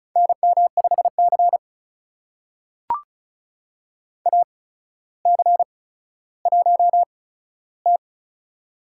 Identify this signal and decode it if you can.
Morse code, heard as a keyed tone, about 36 wpm, spaced as for 15 wpm, 700 Hz. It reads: NM5C AC1T
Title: USA Call Signs - 4 Characters - 8x Character Spacing - 35wpm